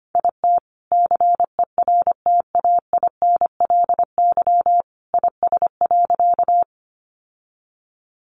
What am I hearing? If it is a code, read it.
Morse code, heard as a keyed tone, about 25 wpm, 700 Hz. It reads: IT CERTAINLY IS.
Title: Sentences from Top 900 Words 25wpm